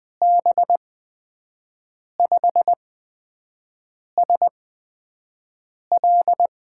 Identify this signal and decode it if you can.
Morse code, heard as a keyed tone, about 20 wpm, spaced as for 5 wpm, 700 Hz. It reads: B5SL
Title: USA Call Signs - 5 Characters - 8x Character Spacing - 20wpm